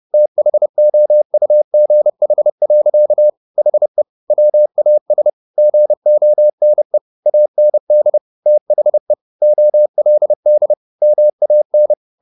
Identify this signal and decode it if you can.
Morse code, heard as a keyed tone, about 30 wpm, 600 Hz. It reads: THOUGH. HE WAS GONE AND THE OLD MAN
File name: the old man and the sea / the old man and the sea 1 30wpm